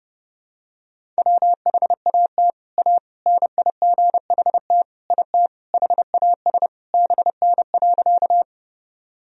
Morse code, heard as a keyed tone, about 32 wpm, 700 Hz. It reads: WHAT A NIGHT IT HAS BN.
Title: Sentences from Top 300 Words and Abbreviations 30wpm